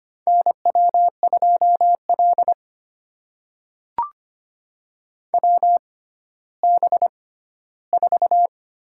Morse code, heard as a keyed tone, about 25 wpm, spaced as for 14 wpm, 700 Hz. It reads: NW2L WB4
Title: USA Call Signs - 6x Character Spacing - 25wpm